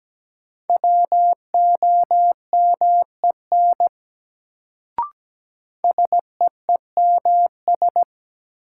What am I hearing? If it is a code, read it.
Morse code, heard as a keyed tone, about 17 wpm, 700 Hz. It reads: WOMEN SEEMS
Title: Sets of 3 Words - New Word Drill - 17wpm